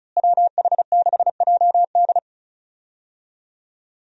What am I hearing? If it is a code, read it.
Morse code, heard as a keyed tone, about 35 wpm, 700 Hz. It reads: WH6JD
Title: USA Call Signs - 5 Characters - Repeated 3x - 35wpm